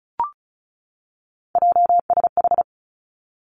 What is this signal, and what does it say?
Morse code, heard as a keyed tone, about 36 wpm, 700 Hz. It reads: JSH
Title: Three Challenging Letter-Number Combinations 35wpm